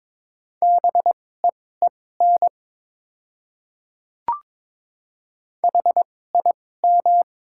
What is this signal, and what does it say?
Morse code, heard as a keyed tone, about 22 wpm, spaced as for 16 wpm, 700 Hz. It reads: BEEN HIM
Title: 2x Character Spacing - Top 100 Words - 22wpm